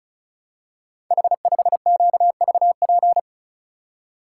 Morse code, heard as a keyed tone, about 36 wpm, 700 Hz. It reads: H5QVP
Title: Code Groups - 5 Letter:Numbers - 35wpm